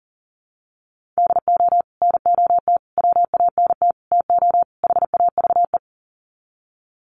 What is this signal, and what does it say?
Morse code, heard as a keyed tone, about 40 wpm, 700 Hz. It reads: DO NOT WANT TO HAVE
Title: Sets of 5 Words 40wpm